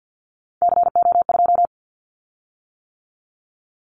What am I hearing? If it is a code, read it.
Morse code, heard as a keyed tone, about 50 wpm, 700 Hz. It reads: CO2
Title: Callsign Prefixes for Top 100 DXCC - 50wpm